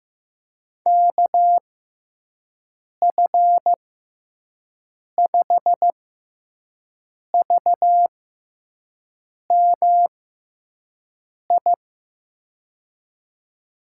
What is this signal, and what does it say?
Morse code, heard as a keyed tone, about 16 wpm, spaced as for 5 wpm, 700 Hz. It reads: KF5VMI
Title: USA Call Signs - 6 Characters - 6x Character Spacing - 15wpm